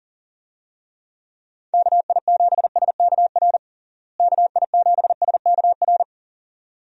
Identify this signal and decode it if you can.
Morse code, heard as a keyed tone, about 40 wpm, 700 Hz. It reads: KI7SKR KI7SKR
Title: USA Call Signs - 6 Characters - Repeated 2x - 40wpm